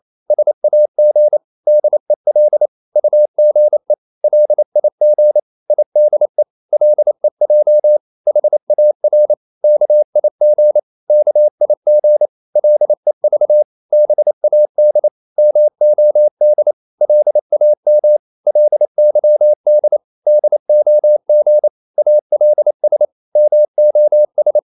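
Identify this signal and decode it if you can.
Morse code, heard as a keyed tone, about 28 wpm, 600 Hz. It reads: SAG DEL UGE LIG IDE LEJ HAR KIG KIG LEV BAD MOD LAM LYD DOG ALS MOS
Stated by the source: Danish short words sent in Morse at 28 wpm, from CW med Gnister.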